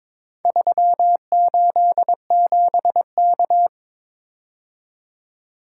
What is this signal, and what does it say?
Morse code, heard as a keyed tone, about 22 wpm, 700 Hz. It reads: 387K
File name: practice / 22wpm / Four Letter-Number Combinations - 22wpm